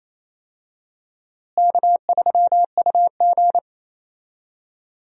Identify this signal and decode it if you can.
Morse code, heard as a keyed tone, about 28 wpm, 700 Hz. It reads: K3UG
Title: USA Call Signs - 4 Characters - Repeated 3x - 28wpm